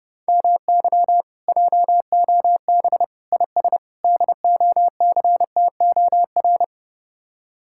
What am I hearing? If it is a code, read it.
Morse code, heard as a keyed tone, about 30 wpm, 700 Hz. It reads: MY JOB IS DOCTOR